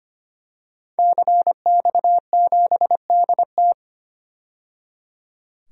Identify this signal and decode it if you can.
Morse code, heard as a keyed tone, about 25 wpm, 700 Hz. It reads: CX7DT
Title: DXCC-100 Call Signs - 5 Characters - 25wpm